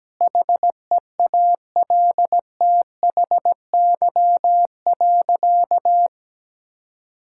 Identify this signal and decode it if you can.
Morse code, heard as a keyed tone, about 17 wpm, 700 Hz. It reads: HEALTHY.
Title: Sentences from Top 1000 Words - Review Part 5 - 17wpm